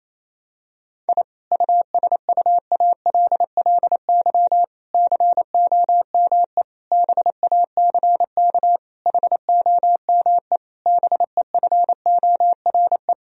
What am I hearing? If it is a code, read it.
Morse code, heard as a keyed tone, about 28 wpm, 700 Hz. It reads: I USUALLY COME BACK HOME BEFORE
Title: Sentences from Top 1000 Words - Review Part 3 - 28wpm